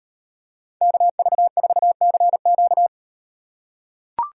Morse code, heard as a keyed tone, about 38 wpm, 700 Hz. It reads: KV4CQ